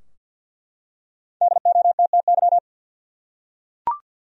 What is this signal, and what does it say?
Morse code, heard as a keyed tone, about 51 wpm, 700 Hz. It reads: DOTTY
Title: USA Common Names - Max 5 Letters - 50wpm